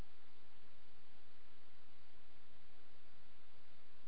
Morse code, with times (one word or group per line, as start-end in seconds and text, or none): none